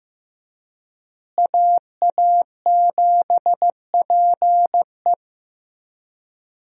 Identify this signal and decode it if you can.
Morse code, heard as a keyed tone, about 15 wpm, 700 Hz. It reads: AA7PE
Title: USA Call Signs - 5 Characters - Repeated 3x - 15wpm